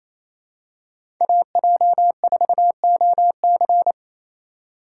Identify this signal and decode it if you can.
Morse code, heard as a keyed tone, about 28 wpm, 700 Hz. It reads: AJ4OC
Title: USA Call Signs - 3x Character Spacing - 28wpm